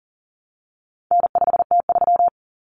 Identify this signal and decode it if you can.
Morse code, heard as a keyed tone, about 40 wpm, 700 Hz. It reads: N5T3
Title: Common Cut Numbers - 4-Digit 40wpm